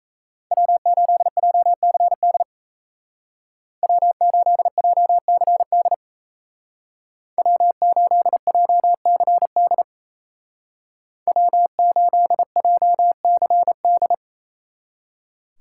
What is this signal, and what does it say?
Morse code, heard as a keyed tone, about 37 wpm, 700 Hz. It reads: W8JCD W8JCD W8JCD W8JCD